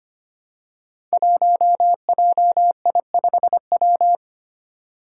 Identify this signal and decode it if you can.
Morse code, heard as a keyed tone, about 25 wpm, 700 Hz. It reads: 1JI5W
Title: Five Challenging Letter-Number Combinations 25wpm